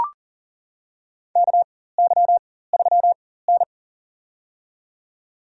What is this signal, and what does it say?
Morse code, heard as a keyed tone, about 40 wpm, spaced as for 19 wpm, 700 Hz. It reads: KY3N
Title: USA Call Signs - 4 Characters - 4x Character Spacing - 40wpm